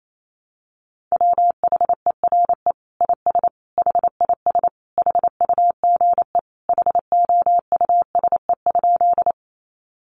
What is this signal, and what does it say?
Morse code, heard as a keyed tone, about 28 wpm, 700 Hz. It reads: WHERE IS HIS HUGE HOUSE?
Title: Sentences from Top 1100 Words 28wpm